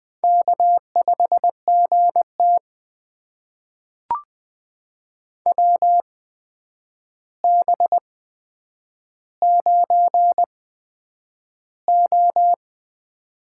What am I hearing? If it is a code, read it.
Morse code, heard as a keyed tone, about 20 wpm, spaced as for 9 wpm, 700 Hz. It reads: K5GT WB9O
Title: USA Call Signs - 4 Characters - 8x Character Spacing - 20wpm